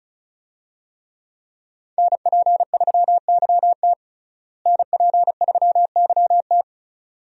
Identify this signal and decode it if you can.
Morse code, heard as a keyed tone, about 36 wpm, 700 Hz. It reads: NP3YT NP3YT
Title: USA Call Signs - 5 Characters - Repeated 2x - 35wpm